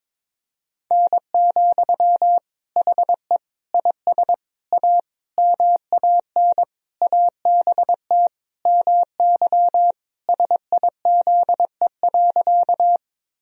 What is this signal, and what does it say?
Morse code, heard as a keyed tone, about 22 wpm, 700 Hz. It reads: N, HE IS A MAN ABT MY SIZE.
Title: Sentences from Top 400 Words and Abbreviations 22wpm